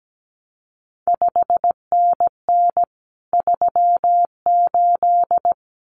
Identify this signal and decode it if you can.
Morse code, heard as a keyed tone, about 17 wpm, 700 Hz. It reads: 5NN 38
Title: CQ WW DX Contest Exchange Encore - 17wpm